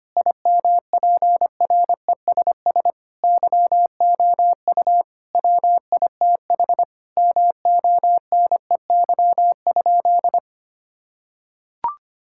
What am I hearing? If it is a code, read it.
Morse code, heard as a keyed tone, about 25 wpm, 700 Hz. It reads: IMPRESS YOU WITH MONEY?
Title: Sentences from Top 1600 Words 25wpm